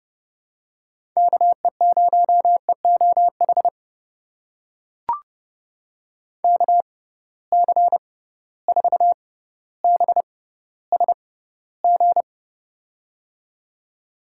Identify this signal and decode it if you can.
Morse code, heard as a keyed tone, about 30 wpm, spaced as for 15 wpm, 700 Hz. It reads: KE0EOH KC4BSG
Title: USA Call Signs - 6 Characters - 6x Character Spacing - 30wpm